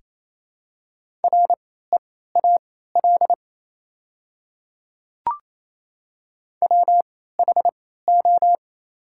Morse code, heard as a keyed tone, about 28 wpm, spaced as for 16 wpm, 700 Hz. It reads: REAL WHO
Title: 3x Character Spacing - Top 300 Words - 28wpm